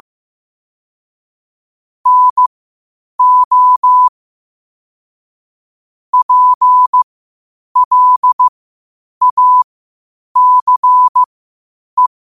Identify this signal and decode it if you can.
Morse code, heard as a keyed tone, about 15 wpm, spaced as for 8 wpm, 1000 Hz. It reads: NO PLACE